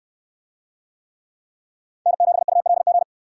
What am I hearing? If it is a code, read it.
Morse code, heard as a keyed tone, about 68 wpm, 700 Hz. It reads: N7ULG